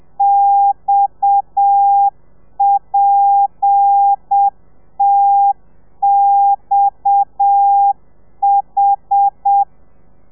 Morse code, heard as a keyed tone, about 7 wpm, 800 Hz. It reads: XPTXH
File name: telegrafia / i2tfj / Lez7 30